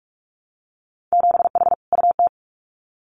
Morse code, heard as a keyed tone, about 46 wpm, 700 Hz. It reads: 7H UT
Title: Winter Field Day Exchanges Encore 45wpm